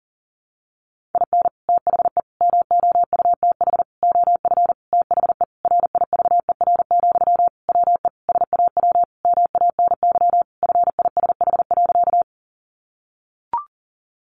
Morse code, heard as a keyed tone, about 40 wpm, 700 Hz. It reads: IN THE MOUTH OF THE RIVER, WE SAW MANY FISH.
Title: Sentences from Top 1500 Words 40wpm